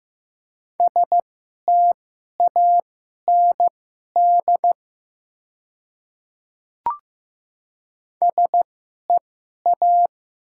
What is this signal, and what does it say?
Morse code, heard as a keyed tone, about 15 wpm, spaced as for 11 wpm, 700 Hz. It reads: STAND SEA
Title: 2x Character Spacing - Top 300 Words - 15wpm